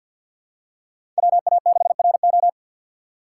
Morse code, heard as a keyed tone, about 47 wpm, 700 Hz. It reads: WA6RO
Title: USA Call Signs - 5 Characters - 6x Character Spacing - 50wpm